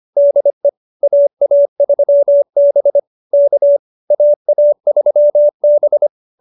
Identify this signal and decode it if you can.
Morse code, heard as a keyed tone, about 25 wpm, 570 Hz. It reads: DE AA3B K AA3B
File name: QSO302 25